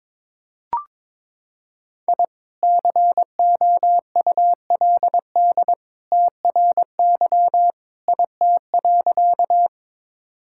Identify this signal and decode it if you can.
Morse code, heard as a keyed tone, about 22 wpm, 700 Hz. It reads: I COULD TRY IT.
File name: practice / 22wpm / Sentences with 2-Word Phrases - 22wpm